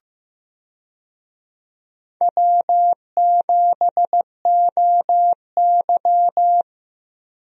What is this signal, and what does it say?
Morse code, heard as a keyed tone, about 15 wpm, 700 Hz. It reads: W7OY
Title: USA Call Signs - 4 Characters - Repeated 3x - 15wpm